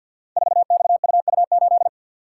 Morse code, heard as a keyed tone, about 47 wpm, 700 Hz. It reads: 4XUU8
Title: Code Groups - 5 Letter:Numbers - 50wpm